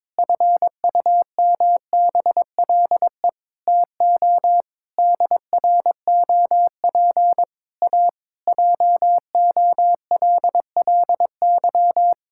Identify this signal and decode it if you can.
Morse code, heard as a keyed tone, about 22 wpm, 700 Hz. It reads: FUMBLE TO DROP A JOLLY
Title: Pangram Challenge v2 - 22wpm